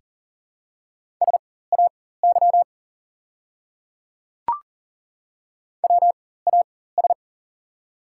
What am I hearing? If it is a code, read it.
Morse code, heard as a keyed tone, about 37 wpm, spaced as for 18 wpm, 700 Hz. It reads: SAY WAS
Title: 4x Character Spacing - Top 200 Words - 40wpm